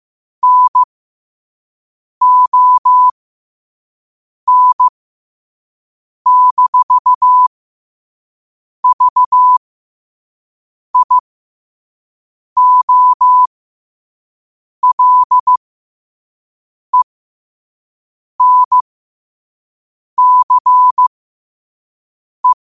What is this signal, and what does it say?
Morse code, heard as a keyed tone, about 15 wpm, spaced as for 5 wpm, 1000 Hz. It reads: NON-VIOLENCE
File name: qotdfiles / 5wpm